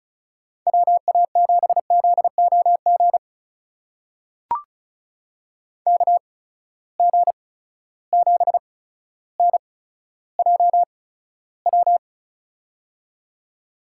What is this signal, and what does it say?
Morse code, heard as a keyed tone, about 36 wpm, spaced as for 15 wpm, 700 Hz. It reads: WA7ZOG KG7NJW
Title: USA Call Signs - 6 Characters - 8x Character Spacing - 35wpm